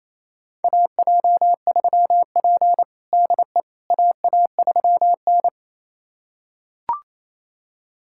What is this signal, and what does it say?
Morse code, heard as a keyed tone, about 28 wpm, 700 Hz. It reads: AJ3P DE AA3N
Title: USA Callsign DE Callsign - 4 Characters - 28wpm